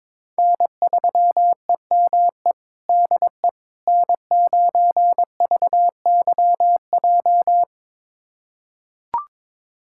Morse code, heard as a keyed tone, about 22 wpm, 700 Hz. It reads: N3EME DE N9VYJ
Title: USA Callsign DE Callsign - 5 Characters - 22wpm